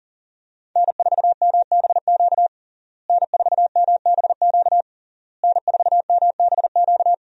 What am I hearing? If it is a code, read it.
Morse code, heard as a keyed tone, about 40 wpm, 700 Hz. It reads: N4MBQ N4MBQ N4MBQ
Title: USA Call Signs - 5 Characters - Repeated 3x - 40wpm